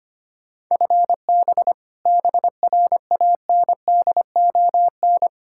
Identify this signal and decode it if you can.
Morse code, heard as a keyed tone, about 25 wpm, 700 Hz. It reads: FB BRANDON
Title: Practice QSO - Exchange 3 - More Info - 25wpm